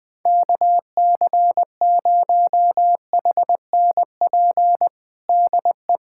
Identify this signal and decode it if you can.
Morse code, heard as a keyed tone, about 20 wpm, 700 Hz. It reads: KC0HNP DE
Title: USA Callsign DE Callsign 20wpm